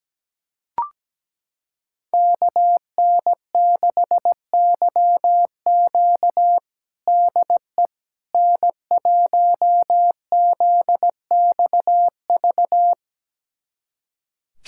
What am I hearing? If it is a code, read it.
Morse code, heard as a keyed tone, about 17 wpm, 700 Hz. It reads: KN6YQ DE N1ZXV
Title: USA Callsign DE Callsign - 5 Characters - 17wpm